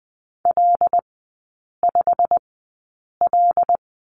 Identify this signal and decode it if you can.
Morse code, heard as a keyed tone, about 20 wpm, spaced as for 8 wpm, 700 Hz. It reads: L5L
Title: 087 Lesson 15e - Number 5 - Three Characters 20wpm